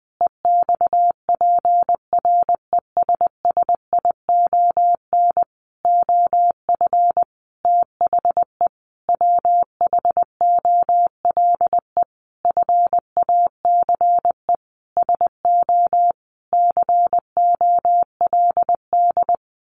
Morse code, heard as a keyed tone, about 19 wpm, 700 Hz. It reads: EXPRESSION OF THE WHOLE FACE SO COLD
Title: Sentences from Top 1400 Words 20wpm